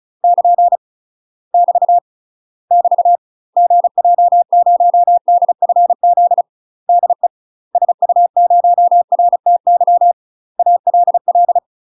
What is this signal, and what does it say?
Morse code, heard as a keyed tone, about 34 wpm, 700 Hz. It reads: <KN> <BT> <BT> GJ0DFZ DE SU0RTY ALL
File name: UK MASTER QSOs TRAINING / 35WPM / UK MASTER QSOs 0003